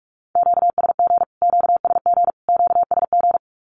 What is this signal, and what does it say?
Morse code, heard as a keyed tone, about 44 wpm, 700 Hz. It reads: QSG QSG QSG
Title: Q-Codes Repeated 3x 45wpm